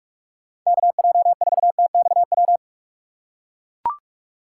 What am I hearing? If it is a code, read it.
Morse code, heard as a keyed tone, about 41 wpm, 700 Hz. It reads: KJ4TXW
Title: USA Call Signs - 6 Characters - 4x Character Spacing - 45wpm